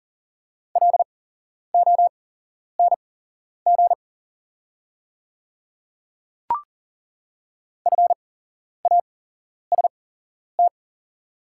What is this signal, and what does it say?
Morse code, heard as a keyed tone, about 40 wpm, spaced as for 11 wpm, 690 Hz. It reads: LONG FAST
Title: 8x Character Spacing - Top 500 Words - 40wpm